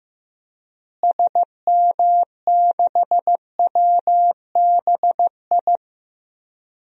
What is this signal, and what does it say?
Morse code, heard as a keyed tone, about 15 wpm, 700 Hz. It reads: SM6WBI
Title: DXCC-100 Call Signs - 6 Characters - Repeated 2x - 15wpm